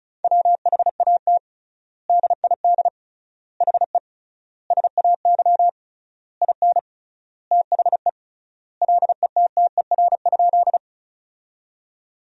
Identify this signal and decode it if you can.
Morse code, heard as a keyed tone, about 36 wpm, 700 Hz. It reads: WHAT DID HE SAY IN THE LETTER?
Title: 3x Word-Spacing - Sentences from Top 300 Words - 35wpm